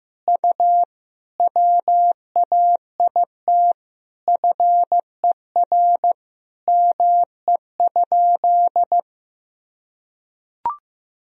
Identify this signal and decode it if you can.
Morse code, heard as a keyed tone, about 15 wpm, 700 Hz. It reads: U WAIT FER ME?
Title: Sentences from Top 500 Words and Abbreviations 15wpm